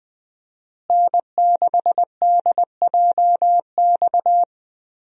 Morse code, heard as a keyed tone, about 20 wpm, 700 Hz. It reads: N6DJX
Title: USA Call Signs - 5 Characters - Repeated 2x - 20wpm